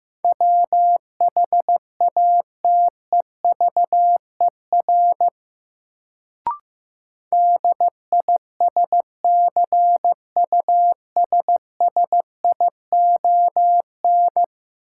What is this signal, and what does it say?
Morse code, heard as a keyed tone, about 15 wpm, 700 Hz. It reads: WHATEVER DISCUSSION
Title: Sentences from Top 1100 Words - New Word Drill - 15wpm